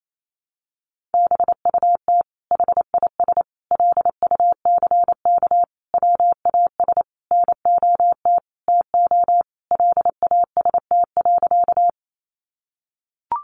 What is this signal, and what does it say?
Morse code, heard as a keyed tone, about 28 wpm, 700 Hz. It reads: BUT HIS LUCK WAS NOT TO LAST.
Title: Sentences from Top 1500 Words 28wpm